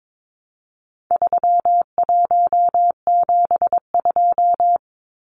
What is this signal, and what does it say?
Morse code, heard as a keyed tone, about 22 wpm, 700 Hz. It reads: 3172
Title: Numbers - 4-Digit 22wpm